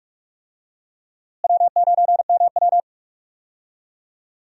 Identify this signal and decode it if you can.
Morse code, heard as a keyed tone, about 42 wpm, 700 Hz. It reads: W9MW
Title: USA Call Signs - 4 Characters - Repeated 2x - 45wpm